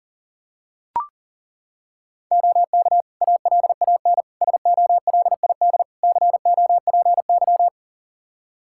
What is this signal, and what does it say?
Morse code, heard as a keyed tone, about 40 wpm, 700 Hz. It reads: OK ALAN SOLID COPY
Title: Practice QSO - Exchange 3 - More Info - 40wpm